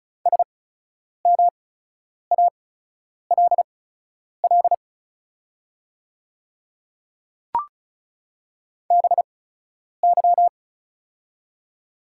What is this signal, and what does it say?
Morse code, heard as a keyed tone, about 36 wpm, spaced as for 10 wpm, 700 Hz. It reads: SMALL BY
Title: 8x Character Spacing - Top 300 Words - 35wpm